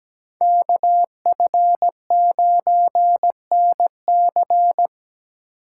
Morse code, heard as a keyed tone, about 17 wpm, 700 Hz. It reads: KF9NC
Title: USA Call Signs - 5 Characters - 2x Character Spacing - 17wpm